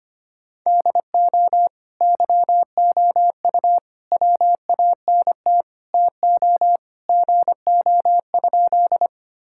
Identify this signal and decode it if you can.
Morse code, heard as a keyed tone, about 25 wpm, 700 Hz. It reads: DO YOU WANT TO GO?
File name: Sentences with 3-Word Phrases - 25wpm